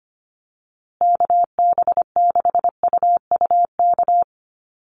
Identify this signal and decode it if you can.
Morse code, heard as a keyed tone, about 25 wpm, 700 Hz. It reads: KB6UUK
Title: USA Call Signs Repeated 3x 25wpm